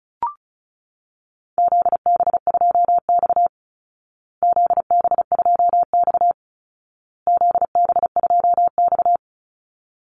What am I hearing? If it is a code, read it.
Morse code, heard as a keyed tone, about 34 wpm, 700 Hz. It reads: ZB2X ZB2X ZB2X
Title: Call Signs Repeated 2x - 35wpm